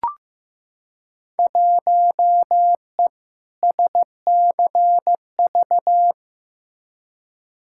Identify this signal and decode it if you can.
Morse code, heard as a keyed tone, about 15 wpm, 700 Hz. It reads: 1E SCV
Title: Field Day Exchanges - 15wpm